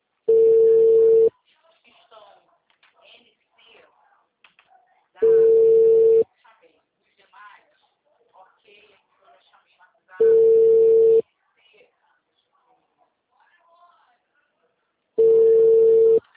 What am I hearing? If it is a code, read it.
Ringback tone (China, 450 Hz), with speech.
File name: out- -2033